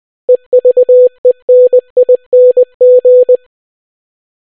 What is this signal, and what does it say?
Morse code, heard as a keyed tone, about 20 wpm, 500 Hz. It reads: EVENING